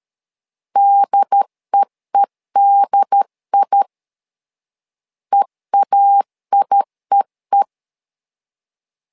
Morse code, heard as a keyed tone, about 13 wpm, 800 Hz. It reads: DEEDI EAIEE
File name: LEZIONI / 2 UHDN / 2 UHD 12wpm 1540